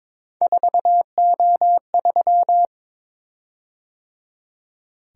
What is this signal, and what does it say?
Morse code, heard as a keyed tone, about 22 wpm, 700 Hz. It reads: 4O3
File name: practice / 22wpm / Three Letter-Number Combinations - 22wpm